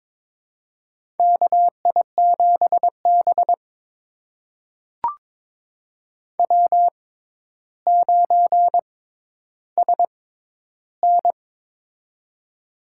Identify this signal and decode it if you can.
Morse code, heard as a keyed tone, about 22 wpm, spaced as for 11 wpm, 700 Hz. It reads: KI7B W9SN
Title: USA Call Signs - 4 Characters - 6x Character Spacing - 22wpm